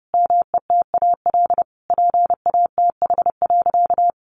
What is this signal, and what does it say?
Morse code, heard as a keyed tone, about 30 wpm, 700 Hz. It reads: METAL PATH.
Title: A Sound Of Thunder 30wpm